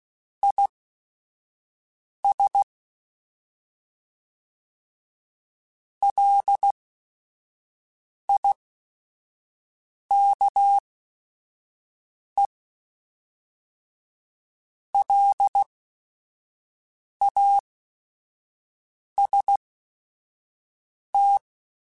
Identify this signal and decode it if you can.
Morse code, heard as a keyed tone, about 16 wpm, spaced as for 5 wpm, 800 Hz. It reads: IS LIKE LAST